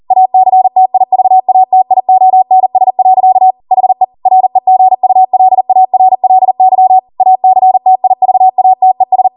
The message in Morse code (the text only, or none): ACTIVATIONS. HE REGULARLY ACTIVATES